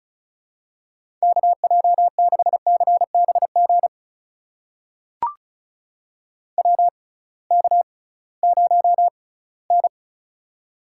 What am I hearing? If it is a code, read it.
Morse code, heard as a keyed tone, about 34 wpm, spaced as for 20 wpm, 700 Hz. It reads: KJ6CBG WK0N
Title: USA Call Signs - 6x Character Spacing - 35wpm